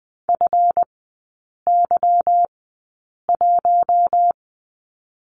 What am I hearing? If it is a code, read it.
Morse code, heard as a keyed tone, about 20 wpm, spaced as for 8 wpm, 700 Hz. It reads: FY1
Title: 218 Lesson 35e - Number 0 - Three Characters 20wpm